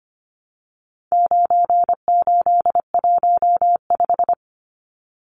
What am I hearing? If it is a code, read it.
Morse code, heard as a keyed tone, about 25 wpm, 700 Hz. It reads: 9815
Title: Numbers - 4-Digit 25wpm